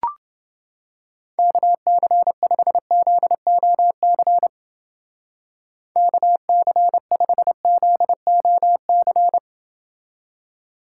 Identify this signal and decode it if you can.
Morse code, heard as a keyed tone, about 28 wpm, 700 Hz. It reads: KC5ZOC KC5ZOC